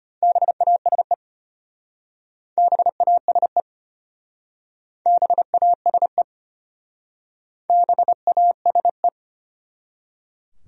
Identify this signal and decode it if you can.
Morse code, heard as a keyed tone, about 34 wpm, 700 Hz. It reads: BASE BASE BASE BASE